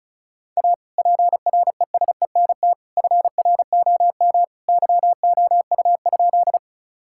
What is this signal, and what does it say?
Morse code, heard as a keyed tone, about 36 wpm, 690 Hz. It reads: A PRESENT FROM YOU?
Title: Sentences from Top 1000 Words - Review Part 5 - 35wpm